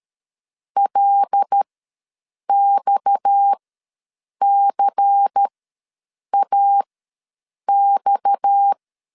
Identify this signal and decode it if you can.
Morse code, heard as a keyed tone, about 13 wpm, spaced as for 7 wpm, 800 Hz. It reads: LXCAX
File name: LEZIONI / 6 PKX0 / 6 PKX ws 12wpm 3106